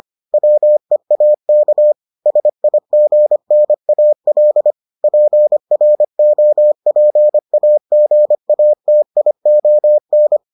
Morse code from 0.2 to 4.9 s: WEAK SIGNAL

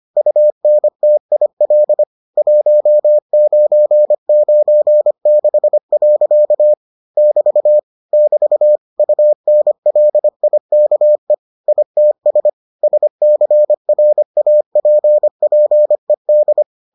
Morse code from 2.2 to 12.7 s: 1996. <BT> <BT> UNLIKE ITS

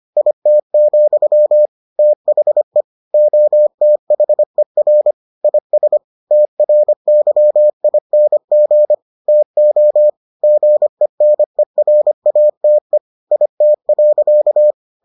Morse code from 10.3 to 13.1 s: GENERATE

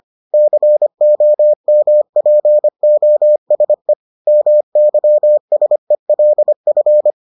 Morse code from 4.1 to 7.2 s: MYSELF

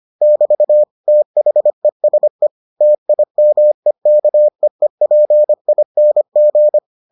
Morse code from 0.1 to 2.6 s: <BT> THESE

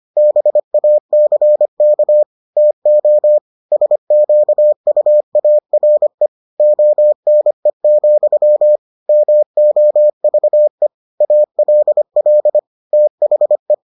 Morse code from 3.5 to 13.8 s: SQUARE ONE, MOVE ALL THE

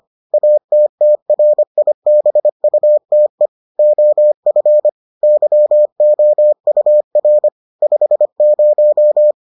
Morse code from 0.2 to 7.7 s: ATTRIBUTE OF YOUR

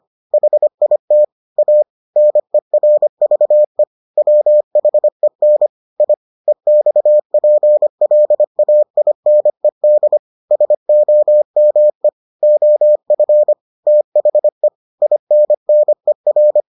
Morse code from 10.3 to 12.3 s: SOME

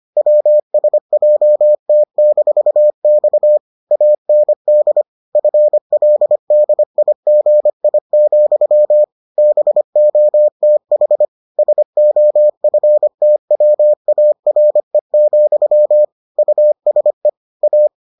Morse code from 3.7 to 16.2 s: AND FLDIGI, BOTH SOFTWARE,

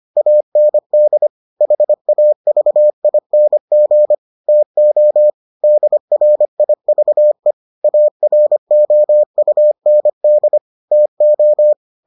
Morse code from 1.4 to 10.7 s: HAVING TO DRIVE AROUND

